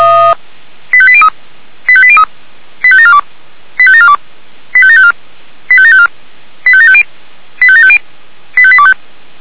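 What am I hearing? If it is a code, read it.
Five-tone selective call (ZVEI), heard as quick runs of five tones, tones of about 70 ms. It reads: 85803 85803 85742 85742 85754 85754 85750 85750 85825
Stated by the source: ZVEI five-tone selective call 85827 received by SDR.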